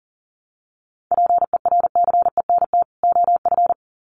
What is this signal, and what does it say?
Morse code, heard as a keyed tone, about 40 wpm, 700 Hz. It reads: PERCENT OF
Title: Sets of 2 Words 40wpm